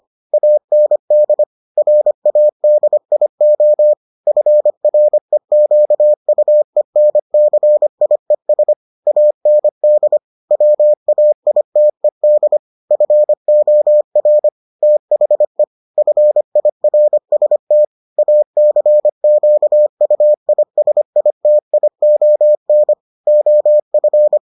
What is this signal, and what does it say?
Morse code, heard as a keyed tone, about 25 wpm, 600 Hz. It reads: AND RADIO FREQUENCIES AND WAITED FOR THE FIRST ACQUISITION OF